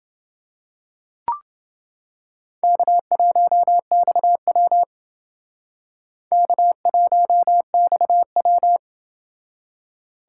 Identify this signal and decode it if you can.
Morse code, heard as a keyed tone, about 28 wpm, 700 Hz. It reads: K1XW K1XW